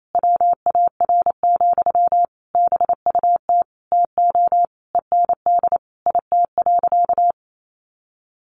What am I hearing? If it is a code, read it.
Morse code, heard as a keyed tone, about 28 wpm, 700 Hz. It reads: WAR, BUT TO END IT.
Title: Sentences from Top 1200 Words 28wpm